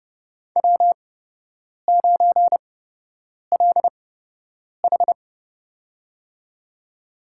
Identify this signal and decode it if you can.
Morse code, heard as a keyed tone, about 30 wpm, spaced as for 8 wpm, 700 Hz. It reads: W9LH
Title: USA Call Signs - 4 Characters - 8x Character Spacing - 30wpm